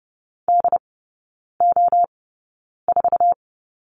Morse code, heard as a keyed tone, about 30 wpm, spaced as for 9 wpm, 700 Hz. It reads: DO4
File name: 081 Lesson 14e - Number 2 - Three Characters 30wpm